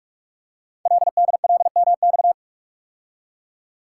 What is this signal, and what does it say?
Morse code, heard as a keyed tone, about 48 wpm, 700 Hz. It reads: LDLMX
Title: Code Groups - 5 Letters - 45wpm